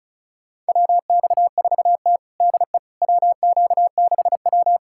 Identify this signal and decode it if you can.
Morse code, heard as a keyed tone, about 36 wpm, 690 Hz. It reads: WX4T DE WQ6W
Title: USA Callsign DE Callsign - 4 Characters - 35wpm